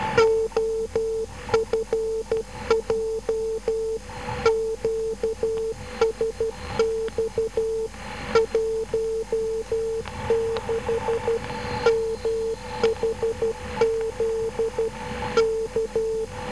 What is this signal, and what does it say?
Morse code, heard as a keyed tone, about 12 wpm, 450 Hz. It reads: OFJ QSX 16 MHZ K
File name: OFJ3 12669 1215